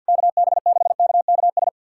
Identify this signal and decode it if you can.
Morse code, heard as a keyed tone, about 52 wpm, 700 Hz. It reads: KB6KKS
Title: USA Callsign DE Callsign - 6 Characters - 50wpm